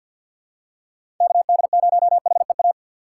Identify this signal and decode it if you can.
Morse code, heard as a keyed tone, about 50 wpm, 700 Hz. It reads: KD0HEA
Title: USA Call Signs - 6x Character Spacing - 50wpm